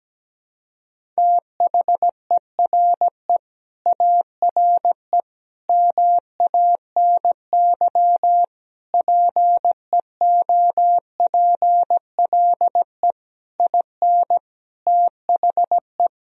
Morse code, heard as a keyed tone, about 17 wpm, 700 Hz. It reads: THERE ARE MANY PEOPLE IN THE
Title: Sentences from Top 1000 Words - Review Part 1 - 17wpm